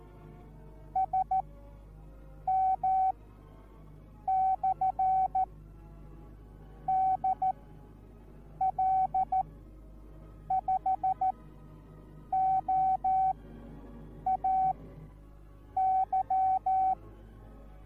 Morse code, heard as a keyed tone, about 14 wpm, spaced as for 6 wpm, 760 Hz. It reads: SM/DL5OAY